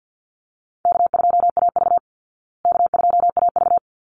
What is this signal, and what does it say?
Morse code, heard as a keyed tone, about 50 wpm, 700 Hz. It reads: K2AV K2AV
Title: Easy Call Signs Repeated 2x - 50wpm